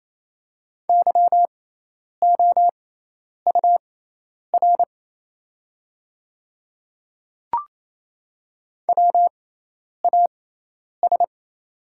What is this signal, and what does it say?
Morse code, heard as a keyed tone, about 28 wpm, spaced as for 10 wpm, 700 Hz. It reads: YOUR WAS
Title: 6x Character Spacing - Top 100 Words - 28wpm